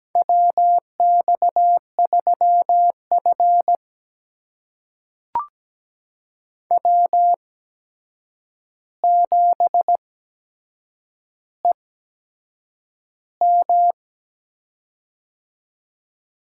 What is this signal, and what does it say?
Morse code, heard as a keyed tone, about 17 wpm, spaced as for 7 wpm, 700 Hz. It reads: WX3F W7EM
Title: USA Call Signs - 8x Character Spacing - 17wpm